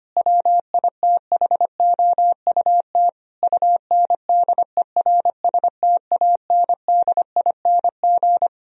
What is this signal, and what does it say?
Morse code, heard as a keyed tone, about 25 wpm, 700 Hz. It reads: WITHOUT UNDERSTANDING
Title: Sentences from Top 1000 Words - Review Part 3 - 25wpm